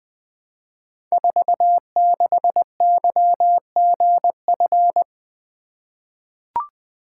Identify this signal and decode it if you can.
Morse code, heard as a keyed tone, about 20 wpm, 700 Hz. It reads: 46YGF
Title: Code Groups - 5 Letter:Numbers - 20wpm